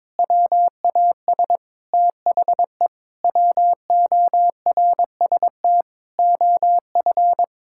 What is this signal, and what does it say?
Morse code, heard as a keyed tone, about 22 wpm, 700 Hz. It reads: WAS THE WORST OF
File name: practice / 22wpm / Sentences with 8-Word Phrases - 22wpm